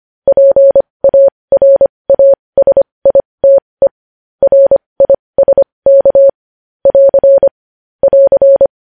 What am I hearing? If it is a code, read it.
Morse code, heard as a keyed tone, about 25 wpm, spaced as for 20 wpm, 550 Hz. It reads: PARASITE RISK <AR> <AR>
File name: podcast / News headlines-1x-25x20